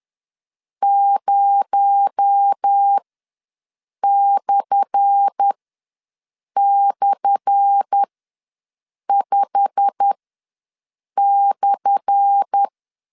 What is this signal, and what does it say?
Morse code, heard as a keyed tone, about 10 wpm, spaced as for 6 wpm, 800 Hz. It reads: 0//5/
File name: LEZIONI / 10 SIGNS / 10A SLASH ws 10wpm 4121